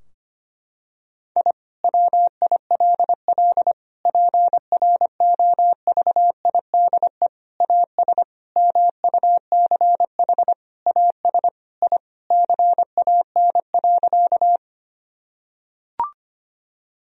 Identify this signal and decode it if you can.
Morse code, heard as a keyed tone, about 25 wpm, 700 Hz. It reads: I WILL PROVIDE AS MUCH AS I CAN.
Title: Sentences from Top 1000 Words - Review Part 3 - 25wpm